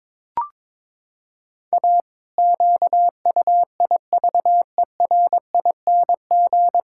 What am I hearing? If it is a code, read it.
Morse code, heard as a keyed tone, about 22 wpm, 700 Hz. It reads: A QUIVERING
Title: Pangram Challenge v2 - 22wpm